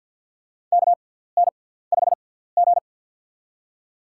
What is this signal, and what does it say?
Morse code, heard as a keyed tone, about 47 wpm, spaced as for 17 wpm, 700 Hz. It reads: KN5G
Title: USA Call Signs - 6x Character Spacing - 50wpm